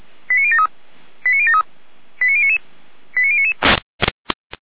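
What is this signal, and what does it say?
Five-tone selective call (ZVEI), heard as quick runs of five tones, tones of about 70 ms. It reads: 89073 89073 89099 89099
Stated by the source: ZVEI five-tone selective call received on an SDR.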